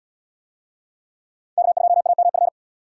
Morse code, heard as a keyed tone, about 72 wpm, 700 Hz. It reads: K2AMF